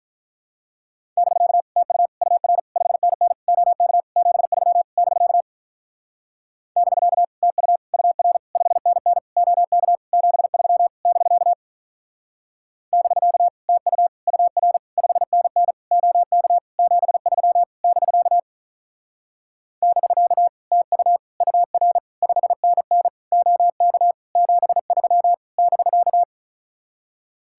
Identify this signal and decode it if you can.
Morse code, heard as a keyed tone, about 44 wpm, 700 Hz. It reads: <BK> TU UR 5NN OK 73 <BK> <BK> TU UR 5NN OK 73 <BK> <BK> TU UR 5NN OK 73 <BK> <BK> TU UR 5NN OK 73 <BK>